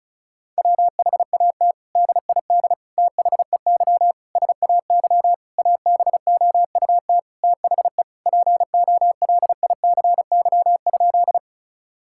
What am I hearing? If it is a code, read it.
Morse code, heard as a keyed tone, about 36 wpm, 700 Hz. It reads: WHAT DID THEY SAY ABOUT THE POLICY?
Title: Sentences from Top 1000 Words - Review Part 5 - 35wpm